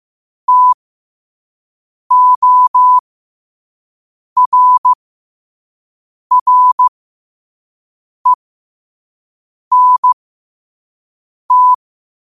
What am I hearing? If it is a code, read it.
Morse code, heard as a keyed tone, about 15 wpm, spaced as for 5 wpm, 1000 Hz. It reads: TORRENT
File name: qotdfiles / 5wpm